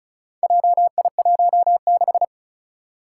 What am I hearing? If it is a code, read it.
Morse code, heard as a keyed tone, about 36 wpm, 700 Hz. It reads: JI16
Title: Four Challenging Letter-Number Combinations 35wpm